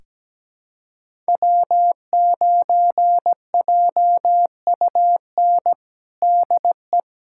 Morse code, heard as a keyed tone, about 17 wpm, 700 Hz. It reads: W9JUN DE
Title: USA Callsign DE Callsign - 5 Characters - 17wpm